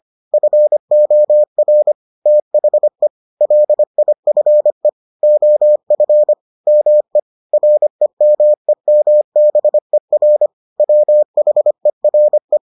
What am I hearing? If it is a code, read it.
Morse code, heard as a keyed tone, about 25 wpm, 600 Hz. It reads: FOR THE LIFE OF ME REMEMBER WHERE